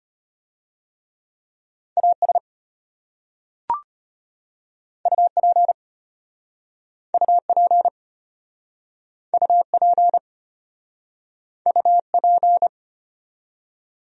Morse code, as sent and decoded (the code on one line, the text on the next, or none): .- ... / ..- .--. / ..- .--. / ..- .--. / ..- .--.
AS UP UP UP UP